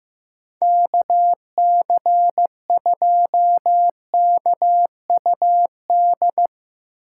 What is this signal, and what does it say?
Morse code, heard as a keyed tone, about 17 wpm, 700 Hz. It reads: KC2KUD
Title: Speed Racing - USA 2x3 Call Signs 15wpm